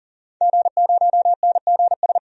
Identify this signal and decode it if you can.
Morse code, heard as a keyed tone, about 40 wpm, 700 Hz. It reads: G0NGS
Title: DXCC-100 Call Signs - 5 Characters - 40wpm